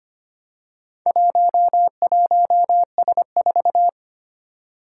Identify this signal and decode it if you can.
Morse code, heard as a keyed tone, about 25 wpm, 700 Hz. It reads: 11S4
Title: Four Challenging Letter-Number Combinations 25wpm